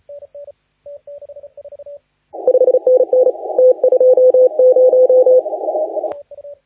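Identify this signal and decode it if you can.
Morse code, heard as a keyed tone, about 29 wpm, 510 Hz. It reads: 5NN T20
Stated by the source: Morse contest audio recorded by LA4C during the SAC CW 2018 contest.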